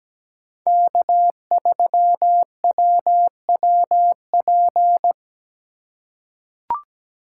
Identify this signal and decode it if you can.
Morse code, heard as a keyed tone, about 17 wpm, 700 Hz. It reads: K3WWP
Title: K1USN SST QSO Practice - 17wpm